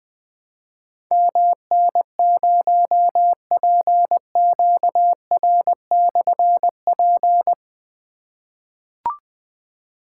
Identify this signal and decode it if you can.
Morse code, heard as a keyed tone, about 20 wpm, 700 Hz. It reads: MN0PQR/P
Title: Difficult Call Signs 20wpm